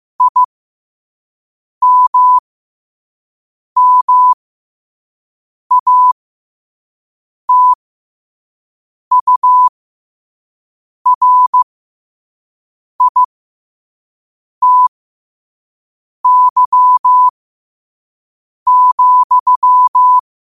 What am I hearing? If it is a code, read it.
Morse code, heard as a keyed tone, about 15 wpm, spaced as for 5 wpm, 1000 Hz. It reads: IMMATURITY,
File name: qotdfiles / 5wpm